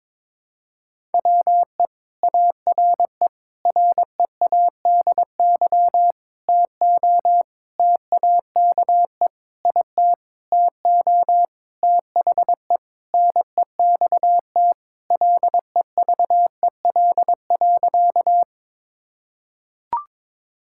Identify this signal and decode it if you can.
Morse code, heard as a keyed tone, about 22 wpm, 700 Hz. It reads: WE ARE READY TO TAKE IT TO THE NEXT LEVEL.
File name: Sentences with 7-Word Phrases - 22wpm